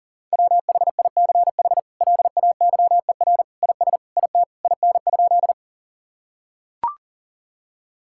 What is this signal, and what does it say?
Morse code, heard as a keyed tone, about 37 wpm, 700 Hz. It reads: WHICH LAYER IS IT IN?
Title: Sentences from Top 1600 Words 40wpm